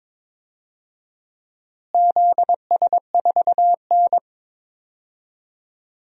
Morse code, heard as a keyed tone, about 22 wpm, 700 Hz. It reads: ZS4N
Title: DXCC-100 Call Signs - 4 Characters - Repeated 3x - 22wpm